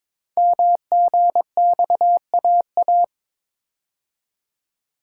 Morse code, heard as a keyed tone, about 22 wpm, 700 Hz. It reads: MGXAA
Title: Code Groups - 5 Letters - 22wpm